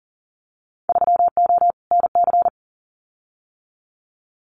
Morse code, heard as a keyed tone, about 41 wpm, 700 Hz. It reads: 3O NC
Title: Winter Field Day Exchanges 40wpm